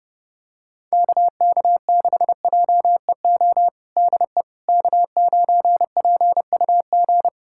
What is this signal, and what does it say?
Morse code, heard as a keyed tone, about 32 wpm, 700 Hz. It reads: KK6JEO DE K9PUG
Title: USA Callsign DE Callsign 30wpm